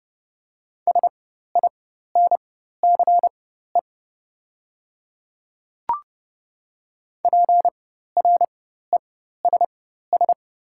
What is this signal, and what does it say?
Morse code, heard as a keyed tone, about 30 wpm, spaced as for 14 wpm, 700 Hz. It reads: SINCE PRESS
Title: 4x Character Spacing - Top 300 Words - 30wpm